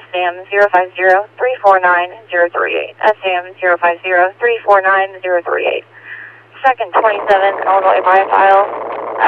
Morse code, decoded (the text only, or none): none